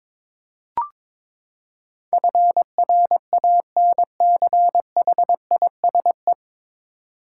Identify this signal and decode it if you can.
Morse code, heard as a keyed tone, about 22 wpm, 700 Hz. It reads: FRANCHISE
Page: Morse Code Ninja practice 22wpm